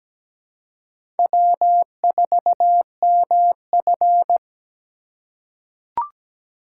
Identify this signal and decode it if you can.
Morse code, heard as a keyed tone, about 17 wpm, 700 Hz. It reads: W4MF
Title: USA Call Signs - 4 Characters - 3x Character Spacing - 17wpm